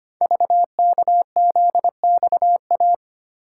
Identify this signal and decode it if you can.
Morse code, heard as a keyed tone, about 25 wpm, 700 Hz. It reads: VKZXA
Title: Code Groups - 5 Letters - 25wpm